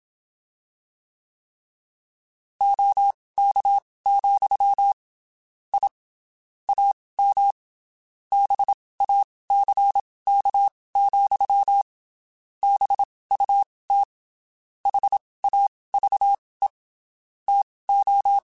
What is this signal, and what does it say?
Morse code, heard as a keyed tone, about 27 wpm, spaced as for 19 wpm, 800 Hz. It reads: OK, I AM BACK, BUT HAVE TO
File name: (spaced) / QSOs Spaced-mp3s / kc7haa @25 sp